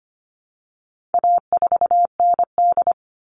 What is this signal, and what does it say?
Morse code, heard as a keyed tone, about 25 wpm, 700 Hz. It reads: A4ND
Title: All Cut Numbers - 4-Digit 25wpm